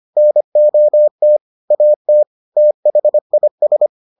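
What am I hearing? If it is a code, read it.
Morse code, heard as a keyed tone, about 25 wpm, 600 Hz. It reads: NOT AT THIS
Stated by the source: Morse edition of the Foundations of Amateur Radio episodes.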